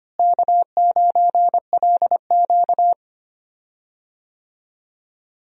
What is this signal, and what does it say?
Morse code, heard as a keyed tone, about 25 wpm, 700 Hz. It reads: K9LQ
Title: USA Call Signs - 4 Characters - Repeated 3x - 25wpm